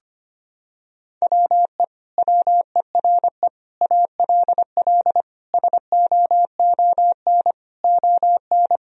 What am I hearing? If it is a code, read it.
Morse code, heard as a keyed tone, about 25 wpm, 700 Hz. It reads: WE WERE ALL SOON ON
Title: Sentences from Top 1000 Words - Review Part 4 - 25wpm